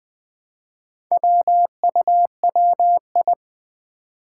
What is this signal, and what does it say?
Morse code, heard as a keyed tone, about 20 wpm, 700 Hz. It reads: WUWI
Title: Four Challenging Letter-Number Combinations 20wpm